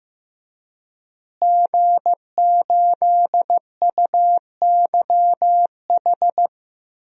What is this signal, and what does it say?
Morse code, heard as a keyed tone, about 15 wpm, 700 Hz. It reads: G8UYH
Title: DXCC-100 Call Signs - 5 Characters - Repeated 3x - 15wpm